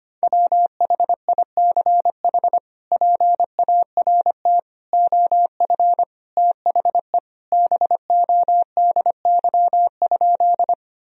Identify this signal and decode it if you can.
Morse code, heard as a keyed tone, about 25 wpm, 700 Hz. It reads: WHICH PART OF THE BODY?
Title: Sentences from Top 1000 Words - Review Part 2 - 25wpm